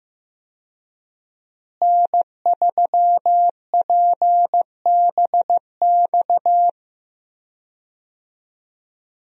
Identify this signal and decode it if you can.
Morse code, heard as a keyed tone, about 15 wpm, 700 Hz. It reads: N3PBX
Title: USA Call Signs - 5 Characters - Repeated 2x - 15wpm